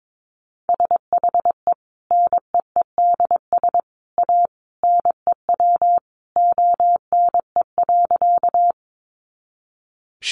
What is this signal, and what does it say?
Morse code, heard as a keyed tone, about 22 wpm, 700 Hz. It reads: SHE NEEDS A NEW ONE.
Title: Additional Easy Phrases 22wpm